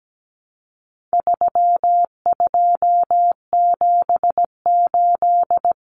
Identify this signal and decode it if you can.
Morse code, heard as a keyed tone, about 17 wpm, 700 Hz. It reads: 3278
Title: Common Cut Numbers - 4-Digit 17wpm